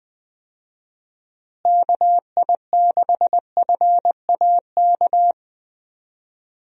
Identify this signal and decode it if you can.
Morse code, heard as a keyed tone, about 20 wpm, 700 Hz. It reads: KI6FAK
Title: USA Call Signs - 6 Characters - Repeated 3x - 20wpm